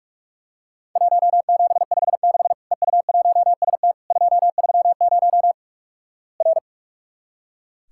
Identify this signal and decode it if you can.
Morse code, heard as a keyed tone, about 46 wpm, 700 Hz. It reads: 1756 EU1ST 230 R
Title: Worked All Europe DX Contest - QTC Exchange - 45wpm